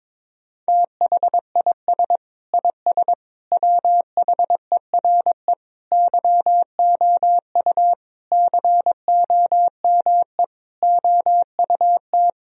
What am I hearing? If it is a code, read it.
Morse code, heard as a keyed tone, about 22 wpm, 700 Hz. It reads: THIS IS WHERE YOU COME OUT
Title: Sentences from Top 1000 Words - Review Part 3 - 22wpm